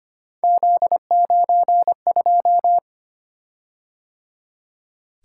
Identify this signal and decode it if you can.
Morse code, heard as a keyed tone, about 25 wpm, 700 Hz. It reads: Z92
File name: Three Letter-Number Combinations - 25wpm